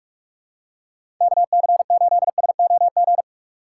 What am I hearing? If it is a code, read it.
Morse code, heard as a keyed tone, about 47 wpm, 700 Hz. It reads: KC8SOG
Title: USA Call Signs - 2x Character Spacing - 45wpm